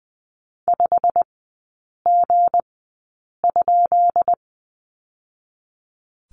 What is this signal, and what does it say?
Morse code, heard as a keyed tone, about 20 wpm, spaced as for 8 wpm, 700 Hz. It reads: 5G?
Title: 159 Lesson 26e - Letter g - Three Characters 20wpm